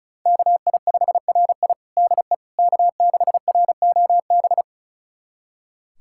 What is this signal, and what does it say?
Morse code, heard as a keyed tone, about 34 wpm, 690 Hz. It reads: KI5RI DE K6ROB
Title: USA Callsign DE Callsign - 5 Characters - 35wpm